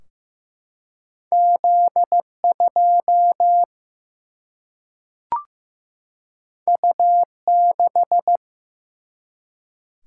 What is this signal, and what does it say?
Morse code, heard as a keyed tone, about 15 wpm, 700 Hz. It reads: Z 2 U 6
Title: Single-Letter Single-Number 15wpm